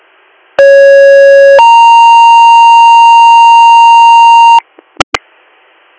Two-tone page, heard as a one-second tone, then a three-second tone, about 550 then 930 Hz. It is heard from 0.6 to 4.6 s.